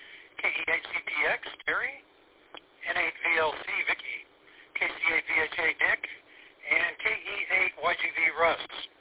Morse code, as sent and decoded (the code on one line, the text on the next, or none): none